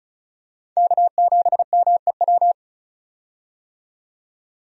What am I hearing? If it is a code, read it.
Morse code, heard as a keyed tone, about 36 wpm, 700 Hz. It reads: K7MEW
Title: USA Call Signs - 5 Characters - 35wpm